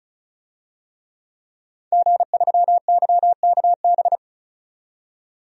Morse code, heard as a keyed tone, about 36 wpm, 700 Hz. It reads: G3YKB
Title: DXCC-100 Call Signs - 5 Characters - 35wpm